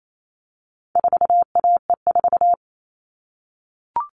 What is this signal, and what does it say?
Morse code, heard as a keyed tone, about 28 wpm, 700 Hz. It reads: 4AE4
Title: All Cut Numbers - 4-Digit 28wpm